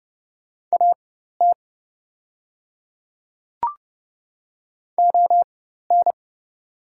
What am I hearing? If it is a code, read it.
Morse code, heard as a keyed tone, about 28 wpm, spaced as for 14 wpm, 700 Hz. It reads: AT ON
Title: 4x Character Spacing - Top 200 Words - 30wpm